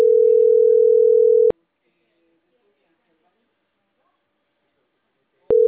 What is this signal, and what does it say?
Ringback tone (North America), 440 + 480 Hz, with speech.
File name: CENTRAL CAEA / out agcaea10